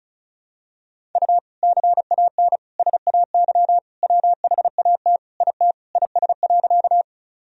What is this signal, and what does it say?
Morse code, heard as a keyed tone, about 35 wpm, 700 Hz. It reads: U CAN SAY WHAT IT IS.
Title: Sentences from Top 200 Words and Abbreviations 35wpm